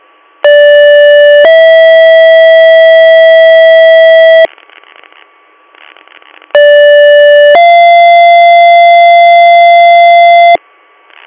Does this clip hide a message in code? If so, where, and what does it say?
Yes: two-tone page from 0.4 to 10.6 s, heard as a one-second tone, then a three-second tone, about 580 then 650 Hz, about 580 then 690 Hz, with noise.